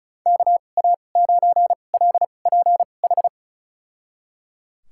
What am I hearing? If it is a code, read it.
Morse code, heard as a keyed tone, about 34 wpm, spaced as for 25 wpm, 690 Hz. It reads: KA9LPH
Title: USA Call Signs - 6 Characters - 2x Character Spacing - 35wpm